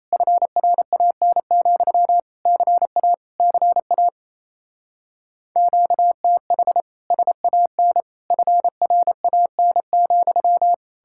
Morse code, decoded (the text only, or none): FRAN, CA CA QTH SAN FRAN,